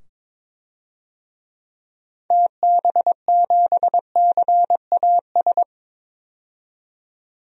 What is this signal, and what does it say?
Morse code, heard as a keyed tone, about 21 wpm, 700 Hz. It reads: TB7CAS
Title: DXCC-100 Call Signs - 6 Characters - 22wpm